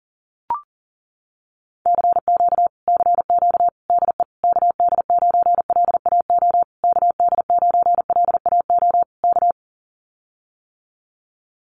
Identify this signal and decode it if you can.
Morse code, heard as a keyed tone, about 40 wpm, 700 Hz. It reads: CQ CQ DE KD9LAO KD9LAO K
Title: Common QSO Phrases Encore 40wpm